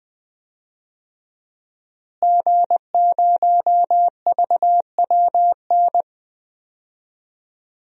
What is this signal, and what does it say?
Morse code, heard as a keyed tone, about 21 wpm, 700 Hz. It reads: G0VWN
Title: DXCC-100 Call Signs - 5 Characters - Repeated 2x - 20wpm